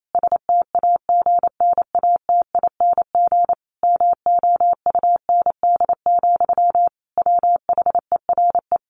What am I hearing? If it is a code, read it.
Morse code, heard as a keyed tone, about 28 wpm, 700 Hz. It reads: STAGNATING MOUND, WHERE
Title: A Sound Of Thunder 28wpm